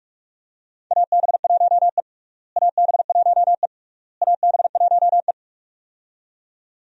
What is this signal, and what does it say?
Morse code, heard as a keyed tone, about 41 wpm, 700 Hz. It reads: AB1E AB1E AB1E
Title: USA Call Signs - 4 Characters - Repeated 3x - 45wpm